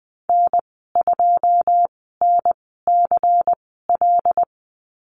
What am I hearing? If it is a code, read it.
Morse code, heard as a keyed tone, about 20 wpm, spaced as for 15 wpm, 700 Hz. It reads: N2NCL
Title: 140 Lesson 23g - Letter f - Call Signs 20wpm